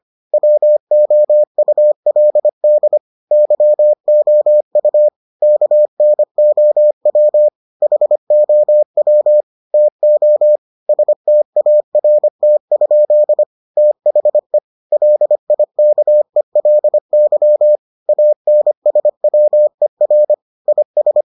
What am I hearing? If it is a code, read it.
Morse code, heard as a keyed tone, about 25 wpm, 600 Hz. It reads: WOULD YOU KNOW HOW TO START? THE LIKELY ANSWER IS